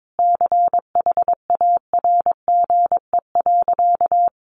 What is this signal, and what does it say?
Morse code, heard as a keyed tone, about 22 wpm, 700 Hz. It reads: CHARGE.
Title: Sentences from Top 1300 Words 22wpm